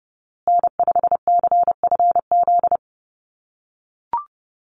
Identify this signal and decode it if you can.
Morse code, heard as a keyed tone, about 30 wpm, 700 Hz. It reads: N5CFZ
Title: USA Call Signs Repeated 2x 30wpm